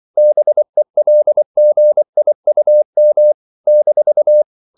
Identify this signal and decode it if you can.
Morse code, heard as a keyed tone, about 24 wpm, 600 Hz. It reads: BELGIUM -
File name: CW / Practice / NationalCapitals24